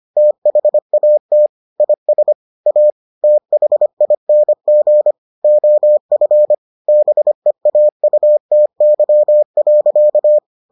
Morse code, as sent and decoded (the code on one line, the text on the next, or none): - .... .- - / .. ... / .- / - .... .. -. --. / --- ..-. / -... .- ..- - -.-- .-.-.-
THAT IS A THING OF BEAUTY.